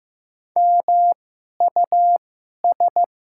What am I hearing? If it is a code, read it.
Morse code, heard as a keyed tone, about 15 wpm, 700 Hz. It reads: M U S
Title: 2x Character Spacing - Top 200 Words - 15wpm